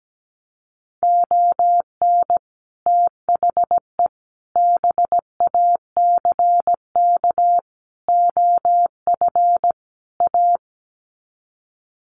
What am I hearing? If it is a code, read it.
Morse code, heard as a keyed tone, about 17 wpm, 700 Hz. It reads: ON THE BACK OF A
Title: Sets of 5 Words Encore2 17wpm